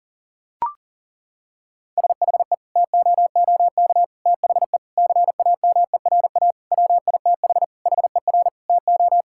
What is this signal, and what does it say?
Morse code, heard as a keyed tone, about 37 wpm, 700 Hz. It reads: SHE TOOK THE CAMERA WITH HER TO
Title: Sentences from Top 1000 Words - Review Part 5 - 40wpm